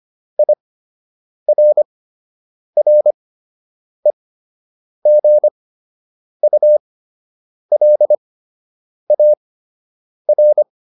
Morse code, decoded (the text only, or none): IRREGULAR